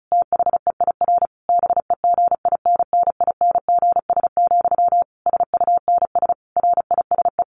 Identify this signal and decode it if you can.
Morse code, heard as a keyed tone, about 34 wpm, 700 Hz. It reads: THEIR BEGINNINGS, SUNS RISE
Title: A Sound Of Thunder 35wpm